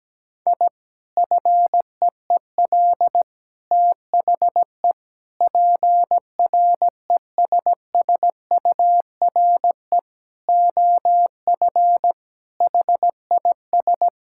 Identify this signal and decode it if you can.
Morse code, heard as a keyed tone, about 17 wpm, 700 Hz. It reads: I FEEL THE PRESSURE OF HIS
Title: Sentences from Top 1000 Words - Review Part 6 - 17wpm